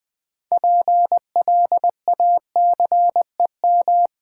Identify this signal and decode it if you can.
Morse code, heard as a keyed tone, about 20 wpm, 700 Hz. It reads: PLACEM
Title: Build A Word - 3 Letters At A Time - 9-Letter Words - 20wpm